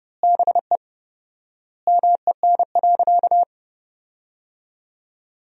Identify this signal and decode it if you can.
Morse code, heard as a keyed tone, about 30 wpm, 700 Hz. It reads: BE MEN.
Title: 4x Word-Spacing - Sentences from Top 300 Words - 30wpm